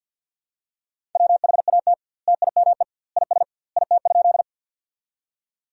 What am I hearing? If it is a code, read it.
Morse code, heard as a keyed tone, about 50 wpm, 700 Hz. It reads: WHAT TIME IS IT?
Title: 2x Word-Spacing - Sentences from Top 100 Words - 50wpm